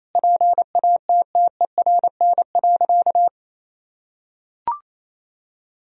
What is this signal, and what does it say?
Morse code, heard as a keyed tone, about 28 wpm, 700 Hz. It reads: PATTERN.
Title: Sentences from Top 1000 Words - Review Part 2 - 28wpm